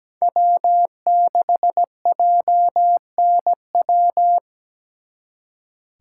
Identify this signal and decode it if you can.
Morse code, heard as a keyed tone, about 20 wpm, 700 Hz. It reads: W6JNW
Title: Speed Racing - USA 1x3 Call Signs 17wpm